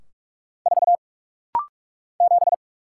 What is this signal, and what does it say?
Morse code, heard as a keyed tone, about 44 wpm, 700 Hz. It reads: 4 7
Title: Single Letter-Number - Rapid-Fire 45wpm